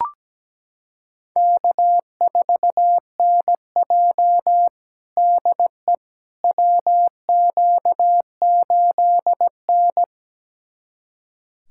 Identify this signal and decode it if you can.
Morse code, heard as a keyed tone, about 17 wpm, 700 Hz. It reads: K4NJ DE WQ8N
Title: USA Callsign DE Callsign - 4 Characters - 17wpm